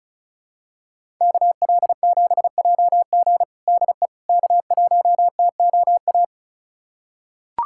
Morse code, heard as a keyed tone, about 36 wpm, 700 Hz. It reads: KL7JG DE K1TOA
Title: USA Callsign DE Callsign - 5 Characters - 35wpm